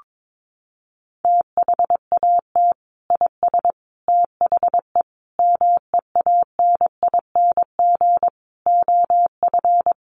Morse code, heard as a keyed tone, about 22 wpm, 700 Hz. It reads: THAT IS THE MEANING OF
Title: Sentences from Top 900 Words 22wpm